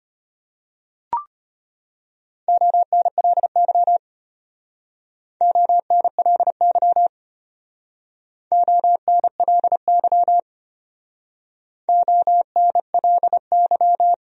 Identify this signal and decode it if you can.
Morse code, heard as a keyed tone, about 31 wpm, 700 Hz. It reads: ONLY ONLY ONLY ONLY